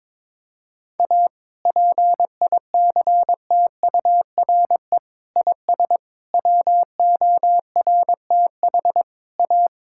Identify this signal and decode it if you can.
Morse code, heard as a keyed tone, about 21 wpm, 700 Hz. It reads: A PICTURE IS WORTH A
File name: practice / 22wpm / Sets of 7 Words - 22wpm